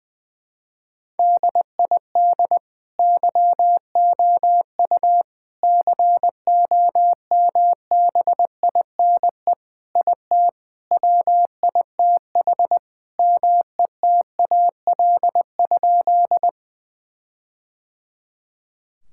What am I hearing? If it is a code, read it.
Morse code, heard as a keyed tone, about 21 wpm, 700 Hz. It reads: DID YOU COMBINE IT WITH METAL?
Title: Sentences from Top 1600 Words 20wpm